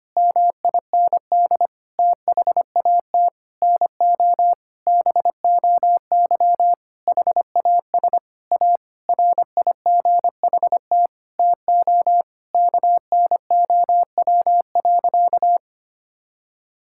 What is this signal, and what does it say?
Morse code, heard as a keyed tone, about 25 wpm, 700 Hz. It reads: MIND THAT NO BOY HAS A RIGHT TO KNOW.
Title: Sentences from Top 1000 Words - Review Part 6 - 25wpm